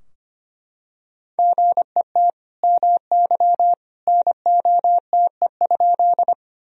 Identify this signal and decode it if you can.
Morse code, heard as a keyed tone, about 25 wpm, 700 Hz. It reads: GET MY NOTE?
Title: Sentences from Top 1000 Words - Review Part 2 - 25wpm